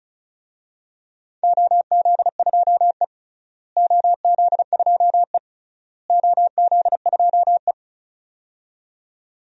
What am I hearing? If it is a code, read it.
Morse code, heard as a keyed tone, about 36 wpm, 700 Hz. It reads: OZ2E OZ2E OZ2E
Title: DXCC-100 Call Signs - 4 Characters - Repeated 3x - 35wpm